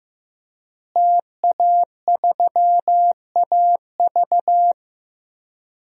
Morse code, heard as a keyed tone, about 15 wpm, 700 Hz. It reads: TA3AV
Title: DXCC-100 Call Signs - 5 Characters - Repeated 2x - 15wpm